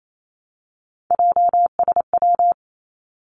Morse code, heard as a keyed tone, about 28 wpm, 700 Hz. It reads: JSW
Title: Three Challenging Letter-Number Combinations 28wpm